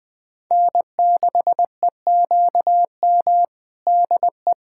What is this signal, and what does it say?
Morse code, heard as a keyed tone, about 20 wpm, 700 Hz. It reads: N6EQM DE
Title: USA Callsign DE Callsign 20wpm